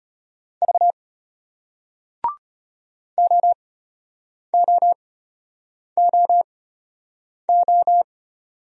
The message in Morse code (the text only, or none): V OOOO